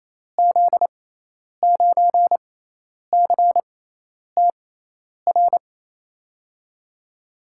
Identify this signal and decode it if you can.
Morse code, heard as a keyed tone, about 28 wpm, spaced as for 10 wpm, 700 Hz. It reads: Z9CTR
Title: USA Call Signs - 6x Character Spacing - 28wpm